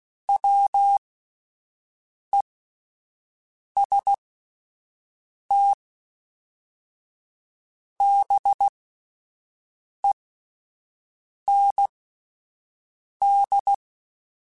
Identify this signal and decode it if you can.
Morse code, heard as a keyed tone, about 16 wpm, spaced as for 5 wpm, 800 Hz. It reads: WEST BEND